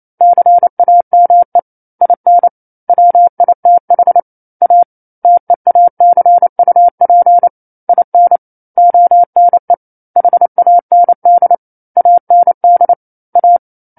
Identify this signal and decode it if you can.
Morse code, heard as a keyed tone, about 28 wpm, 700 Hz. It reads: CAME IN WITH A TEACUP IN ONE HAND AND A
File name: 09-Chapter-09 28 WPM